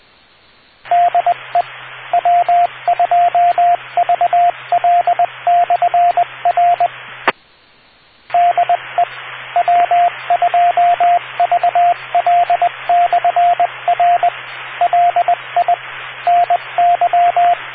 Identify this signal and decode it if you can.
Morse code, heard as a keyed tone, about 20 wpm, 700 Hz, with radio noise. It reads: DE W2VL/R DE W2VL/R LI NY